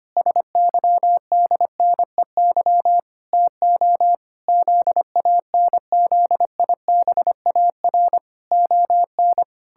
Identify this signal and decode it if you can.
Morse code, heard as a keyed tone, about 25 wpm, 700 Hz. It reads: SYDNEY TO ZANZIBAR ON
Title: Pangram Challenge v2 - 25wpm